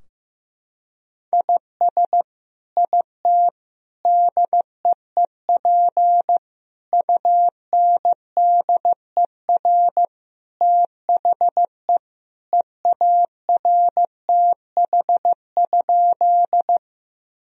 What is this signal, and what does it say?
Morse code, heard as a keyed tone, about 15 wpm, 700 Hz. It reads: IS IT DEEP UNDER THE EARTH?
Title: Sentences from Top 1000 Words - Review Part 3 - 15wpm